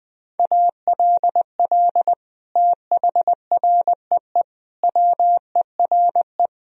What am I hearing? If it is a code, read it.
Morse code, heard as a keyed tone, about 21 wpm, 700 Hz. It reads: ALL THREE WERE
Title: Sentences from Top 1000 Words - Review Part 4 - 20wpm